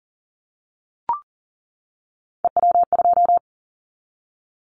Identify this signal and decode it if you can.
Morse code, heard as a keyed tone, about 40 wpm, 700 Hz. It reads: EW2
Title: Callsign Prefixes for Top 50 DXCC - 40wpm